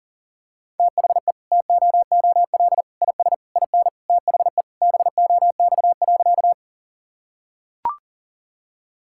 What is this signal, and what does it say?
Morse code, heard as a keyed tone, about 40 wpm, 700 Hz. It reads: THE TOOL IS IN THE BOX.
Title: Sentences from Top 1000 Words - Review Part 6 - 40wpm